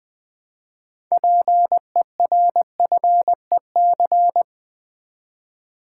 Morse code, heard as a keyed tone, about 20 wpm, 700 Hz. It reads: PERFEC
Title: Build A Word - 3 Letters At A Time - 9-Letter Words - 20wpm